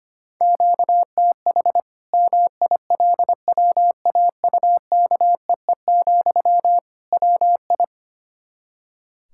Morse code, heard as a keyed tone, about 25 wpm, 700 Hz. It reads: QTH MILWAUKEE, WI
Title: Practice QSO - Exchange 1 - Beginning - 25wpm